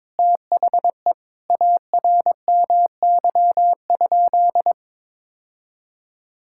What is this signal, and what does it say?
Morse code, heard as a keyed tone, about 21 wpm, 700 Hz. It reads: THE ARMY?
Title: Sentences from Top 1000 Words - Review Part 5 - 22wpm